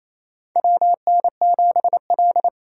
Morse code, heard as a keyed tone, about 27 wpm, 700 Hz. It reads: WN7L
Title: USA Call Signs - 4 Characters - 3x Character Spacing - 28wpm